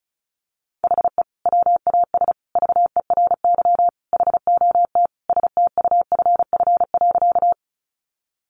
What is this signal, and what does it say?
Morse code, heard as a keyed tone, about 34 wpm, 700 Hz. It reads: HE WAS VERY HOT STUFF.
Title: Sentences from Top 1400 Words 35wpm